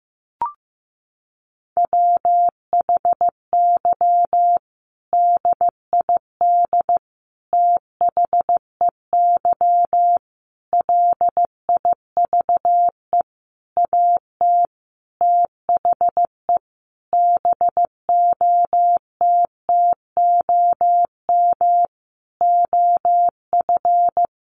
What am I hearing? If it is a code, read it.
Morse code, heard as a keyed tone, about 15 wpm, 700 Hz. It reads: WHY DID THEY LIVE AT THE BOTTOM OF
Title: Sentences from Top 900 Words 15wpm